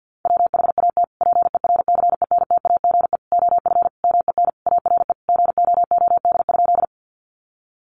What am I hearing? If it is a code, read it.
Morse code, heard as a keyed tone, about 50 wpm, 700 Hz. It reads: WHAT PERCENTAGE OF MEN ARE GOOD?
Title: Sentences from Top 1300 Words 50wpm